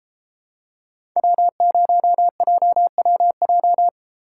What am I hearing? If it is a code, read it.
Morse code, heard as a keyed tone, about 31 wpm, 700 Hz. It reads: W0JWJ